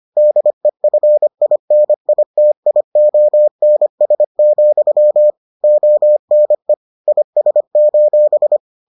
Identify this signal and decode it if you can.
Morse code, heard as a keyed tone, about 25 wpm, 600 Hz. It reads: DEFINITIONS, ONE IS: